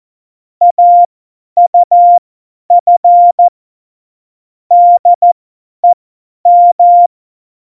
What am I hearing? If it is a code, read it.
Morse code, heard as a keyed tone, about 14 wpm, spaced as for 10 wpm, 700 Hz. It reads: AUF DEM
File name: Morsekurs / Witze / Witz 23. Oktober 2018, 50 F70